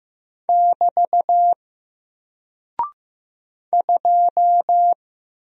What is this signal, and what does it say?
Morse code, heard as a keyed tone, about 15 wpm, 700 Hz. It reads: <BT> 2
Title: Single Character (Letter:Number:Punctuation:Prosign) Alt 15wpm